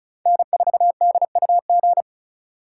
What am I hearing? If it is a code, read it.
Morse code, heard as a keyed tone, about 36 wpm, 700 Hz. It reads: N4DUG
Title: USA Call Signs - 5 Characters - 8x Character Spacing - 35wpm